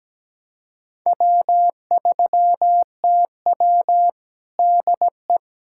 Morse code, heard as a keyed tone, about 17 wpm, 700 Hz. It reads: W3TW DE
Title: K1USN SST QSO Practice 17wpm